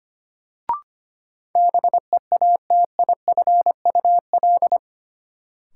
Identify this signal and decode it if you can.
Morse code, heard as a keyed tone, about 25 wpm, 700 Hz. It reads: BEATIFUL
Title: Sentences from Top 1300 Words - New Word Drill Alt - 25wpm